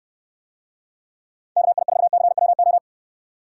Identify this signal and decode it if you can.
Morse code, heard as a keyed tone, about 69 wpm, 700 Hz. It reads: KI4CWO